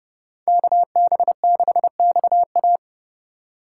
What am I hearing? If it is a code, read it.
Morse code, heard as a keyed tone, about 35 wpm, 700 Hz. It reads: KB6XA
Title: Speed Racing - USA 2x2 Call Signs 30wpm